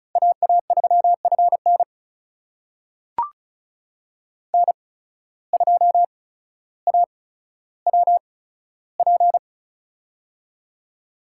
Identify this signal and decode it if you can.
Morse code, heard as a keyed tone, about 34 wpm, spaced as for 15 wpm, 700 Hz. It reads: AA3FN N2AWP
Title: USA Call Signs - 5 Characters - 8x Character Spacing - 35wpm